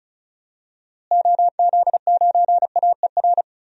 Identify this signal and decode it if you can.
Morse code, heard as a keyed tone, about 36 wpm, 700 Hz. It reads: OZ9AER
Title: DXCC-100 Call Signs - 6 Characters - Repeated 2x - 35wpm